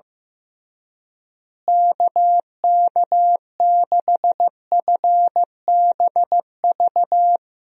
Morse code, heard as a keyed tone, about 15 wpm, 700 Hz. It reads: KK6FBV
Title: USA Call Signs - 6 Characters - Repeated 3x - 15wpm